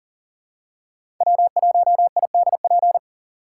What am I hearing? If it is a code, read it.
Morse code, heard as a keyed tone, about 40 wpm, 700 Hz. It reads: W1IDP
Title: USA Call Signs - 5 Characters - 3x Character Spacing - 40wpm